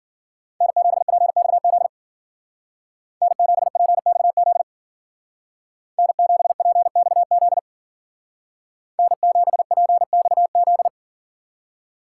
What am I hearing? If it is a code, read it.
Morse code, heard as a keyed tone, about 53 wpm, 700 Hz. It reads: N7PXZ N7PXZ N7PXZ N7PXZ